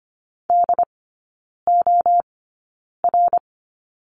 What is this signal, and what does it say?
Morse code, heard as a keyed tone, about 25 wpm, spaced as for 9 wpm, 690 Hz. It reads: DOR
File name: Lesson 15e - Number 5 - Three Characters 25wpm